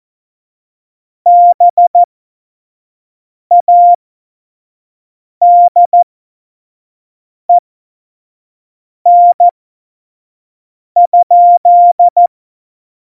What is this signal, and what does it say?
Morse code, heard as a keyed tone, about 14 wpm, spaced as for 5 wpm, 700 Hz. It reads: BADEN?
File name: Morsekurs / Witze / Witz 30. Januar 2018, 25 F70